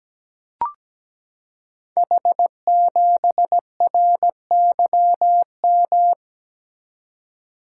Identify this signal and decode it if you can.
Morse code, heard as a keyed tone, about 17 wpm, 700 Hz. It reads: H7RYM
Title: Code Groups - 5 Letter:Numbers - 17wpm